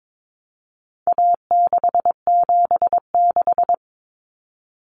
Morse code, heard as a keyed tone, about 22 wpm, 700 Hz. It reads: A676
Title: Common Cut Numbers - 4-Digit 22wpm